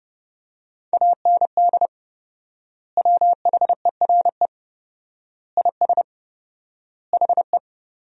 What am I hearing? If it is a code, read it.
Morse code, heard as a keyed tone, about 30 wpm, 700 Hz. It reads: AND WHERE IS HE
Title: 4x Word-Spacing - Sentences from Top 200 Words - 30wpm